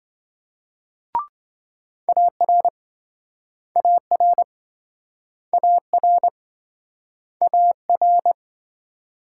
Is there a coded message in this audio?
Yes: Morse code from 2.1 to 8.3 s, heard as a keyed tone, about 27 wpm, 700 Hz. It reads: AR AR AR AR